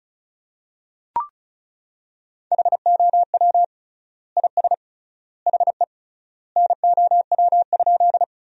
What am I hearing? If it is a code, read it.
Morse code, heard as a keyed tone, about 36 wpm, 700 Hz. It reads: HOW IS HE NOW?
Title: 3x Word-Spacing - Sentences from Top 100 Words - 35wpm